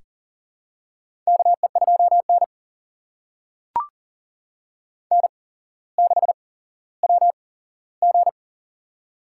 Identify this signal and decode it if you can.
Morse code, heard as a keyed tone, about 40 wpm, spaced as for 17 wpm, 700 Hz. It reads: KE2N N6WG
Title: USA Call Signs - 4 Characters - 8x Character Spacing - 40wpm